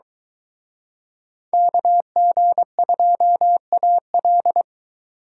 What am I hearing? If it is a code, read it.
Morse code, heard as a keyed tone, about 23 wpm, 700 Hz. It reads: KG2AL